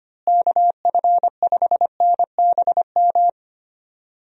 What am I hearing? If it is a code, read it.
Morse code, heard as a keyed tone, about 25 wpm, 700 Hz. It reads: KF5NBM
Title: USA Call Signs - 6 Characters - Repeated 2x - 25wpm